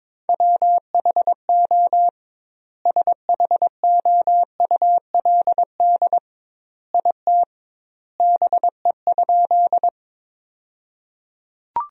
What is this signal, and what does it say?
Morse code, heard as a keyed tone, about 22 wpm, 700 Hz. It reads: WHO SHOULD IT BE?
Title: 2x Word-Spacing - Sentences from Top 300 Words - 22wpm